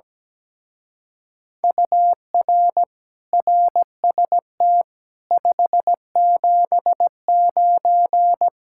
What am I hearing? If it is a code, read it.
Morse code, heard as a keyed tone, about 17 wpm, 700 Hz. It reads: UR RST 579
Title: Speed-Racing Common QSO Phrases - 17wpm